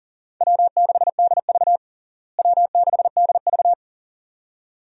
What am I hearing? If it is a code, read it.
Morse code, heard as a keyed tone, about 37 wpm, 700 Hz. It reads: W6DV W6DV
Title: USA Call Signs - 4 Characters - Repeated 2x - 40wpm